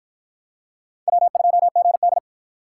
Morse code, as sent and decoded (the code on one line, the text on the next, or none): .-- ..--- --. -..
W2GD